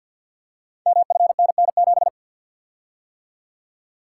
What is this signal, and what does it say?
Morse code, heard as a keyed tone, about 51 wpm, 700 Hz. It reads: MFNN7
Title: Code Groups - 5 Letter:Numbers - 50wpm